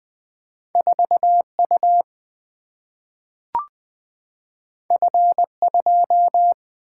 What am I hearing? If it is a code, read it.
Morse code, heard as a keyed tone, about 20 wpm, 700 Hz. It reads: 4U F2
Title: Single-Letter Single-Number 20wpm